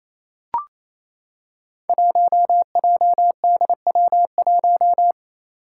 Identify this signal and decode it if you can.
Morse code, heard as a keyed tone, about 28 wpm, 700 Hz. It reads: 1JDW1
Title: Five Challenging Letter-Number Combinations 28wpm